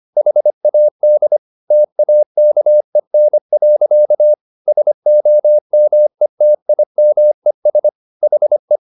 Morse code, heard as a keyed tone, about 25 wpm, 600 Hz. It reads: HAD TAKEN. SOMETIMES HE